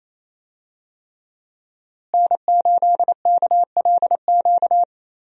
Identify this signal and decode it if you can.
Morse code, heard as a keyed tone, about 28 wpm, 700 Hz. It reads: N8KLQ
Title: USA Call Signs - 5 Characters - Repeated 3x - 28wpm